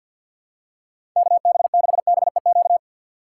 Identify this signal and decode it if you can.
Morse code, heard as a keyed tone, about 49 wpm, 700 Hz. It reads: KB6BEQ
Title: USA Call Signs - 6 Characters - 6x Character Spacing - 50wpm